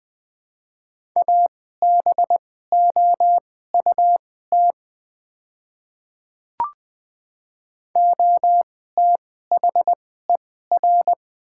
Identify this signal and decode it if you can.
Morse code, heard as a keyed tone, about 20 wpm, spaced as for 14 wpm, 700 Hz. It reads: ABOUT OTHER
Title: 2x Character Spacing - Top 100 Words - 20wpm